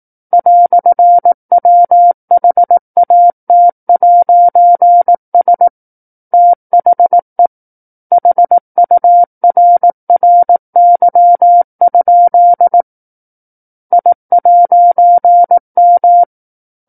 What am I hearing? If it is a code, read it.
Morse code, heard as a keyed tone, about 18 wpm, 700 Hz. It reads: "WHAT'S THE HURRY? I'M
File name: 18 WPM